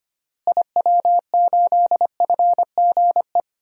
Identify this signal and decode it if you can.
Morse code, heard as a keyed tone, about 25 wpm, 700 Hz. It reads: IW8FGE
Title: DXCC-100 Call Signs - 6 Characters - Repeated 2x - 25wpm